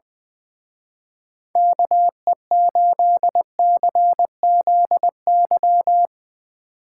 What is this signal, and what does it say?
Morse code, heard as a keyed tone, about 20 wpm, 700 Hz. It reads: KE8CZY
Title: Speed Racing - USA 2x3 Call Signs 20wpm